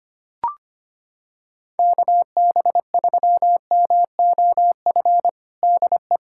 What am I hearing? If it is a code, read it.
Morse code, heard as a keyed tone, about 25 wpm, 700 Hz. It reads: KB3MOF DE
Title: USA Callsign DE Callsign 25wpm